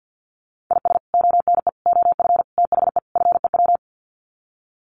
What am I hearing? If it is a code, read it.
Morse code, heard as a keyed tone, about 50 wpm, 700 Hz. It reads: IS ONE OF THE FEW
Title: Sets of 5 Words 50wpm